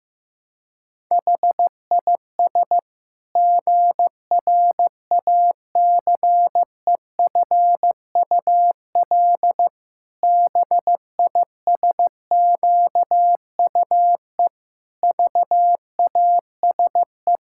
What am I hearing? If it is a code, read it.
Morse code, heard as a keyed tone, about 15 wpm, 700 Hz. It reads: HIS GRACEFUL BISQUE VASE
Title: Pangram Challenge v2 - 15wpm